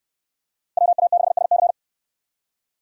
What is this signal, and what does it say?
Morse code, heard as a keyed tone, about 69 wpm, 690 Hz. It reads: WA6SC